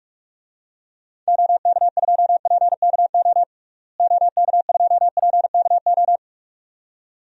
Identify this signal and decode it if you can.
Morse code, heard as a keyed tone, about 45 wpm, 700 Hz. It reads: OK2PKO OK2PKO
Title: DXCC-100 Call Signs - 6 Characters - Repeated 2x - 45wpm